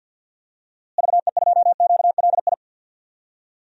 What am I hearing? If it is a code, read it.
Morse code, heard as a keyed tone, about 52 wpm, 700 Hz. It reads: VE2QLI
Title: DXCC-100 Call Signs - 6 Characters - Repeated 3x - 50wpm